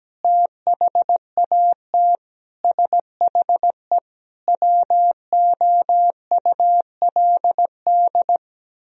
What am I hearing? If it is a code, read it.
Morse code, heard as a keyed tone, about 17 wpm, 700 Hz. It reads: THAT SHE WOULD
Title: Sentences from Top 1000 Words - Review Part 2 - 17wpm